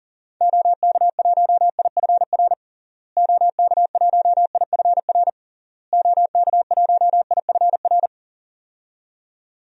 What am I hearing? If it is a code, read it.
Morse code, heard as a keyed tone, about 37 wpm, 700 Hz. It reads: OK1IFR OK1IFR OK1IFR
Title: DXCC-100 Call Signs - 6 Characters - Repeated 3x - 40wpm